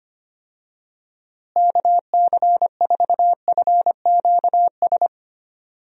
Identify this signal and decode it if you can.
Morse code, heard as a keyed tone, about 25 wpm, 700 Hz. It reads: KC4FQS
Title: USA Call Signs - 6 Characters - Repeated 3x - 25wpm